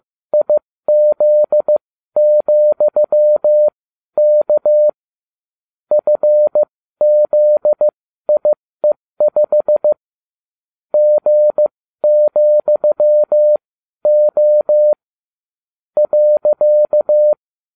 Morse code, heard as a keyed tone, about 15 wpm, spaced as for 12 wpm, 600 Hz. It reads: IZ,K FZIE5 G,O .